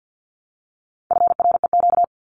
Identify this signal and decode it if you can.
Morse code, heard as a keyed tone, about 50 wpm, 700 Hz. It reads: FREQ
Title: Abbreviations Repeated 3x - 50wpm